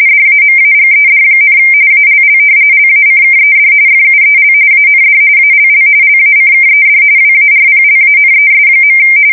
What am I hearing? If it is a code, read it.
RTTY, about 45.45 baud. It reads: DAMAGED, ALL LACKED FUEL AND ALL LACKED WARPLANES.
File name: fsk170 / usnavy4